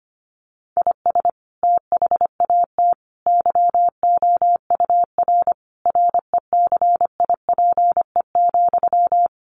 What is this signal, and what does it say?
Morse code, heard as a keyed tone, about 25 wpm, 700 Hz. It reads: IS THAT YOUR RECIPE,
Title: Sentences from Top 1200 Words 25wpm